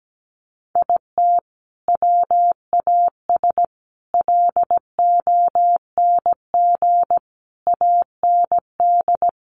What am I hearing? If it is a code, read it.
Morse code, heard as a keyed tone, about 17 wpm, 700 Hz. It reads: IT WAS LONG AND
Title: Sentences from Top 1400 Words 17wpm